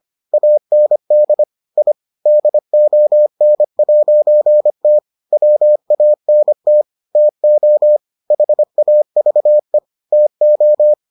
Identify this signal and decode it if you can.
Morse code, heard as a keyed tone, about 25 wpm, 600 Hz. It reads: AND I DON'T WANT TO HAVE TO